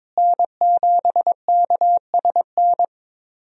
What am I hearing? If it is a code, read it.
Morse code, heard as a keyed tone, about 22 wpm, 700 Hz. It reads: N7KSN
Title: USA Call Signs - 5 Characters - Repeated 2x - 22wpm